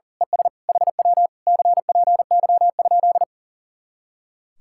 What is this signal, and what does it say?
Morse code, heard as a keyed tone, about 35 wpm, 700 Hz. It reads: ES HW CPY?
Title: Speed-Racing Common QSO Phrases Encore - 40wpm